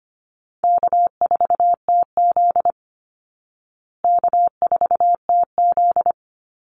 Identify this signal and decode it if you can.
Morse code, heard as a keyed tone, about 25 wpm, 700 Hz. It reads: K4TZ K4TZ
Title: Easy Call Signs Repeated 3x - 25wpm